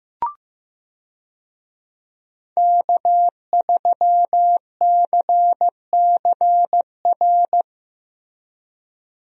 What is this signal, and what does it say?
Morse code, heard as a keyed tone, about 15 wpm, 700 Hz. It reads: K3CCR
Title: USA Call Signs - 5 Characters - Repeated 3x - 15wpm